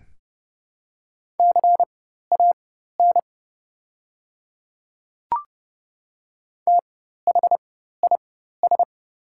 Morse code, heard as a keyed tone, about 32 wpm, spaced as for 14 wpm, 700 Hz. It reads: CAN THIS
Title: 4x Character Spacing - Top 500 Words - 30wpm